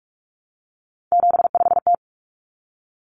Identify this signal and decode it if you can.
Morse code, heard as a keyed tone, about 44 wpm, 700 Hz. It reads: 75T
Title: Common Cut Numbers - 3-Digit 45wpm